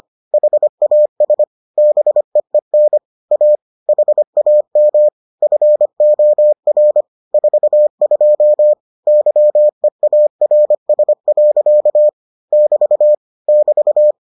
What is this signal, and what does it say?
Morse code, heard as a keyed tone, about 25 wpm, 600 Hz. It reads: HAS BEEN A HAM FOR 42 YEARS. <BT> <BT>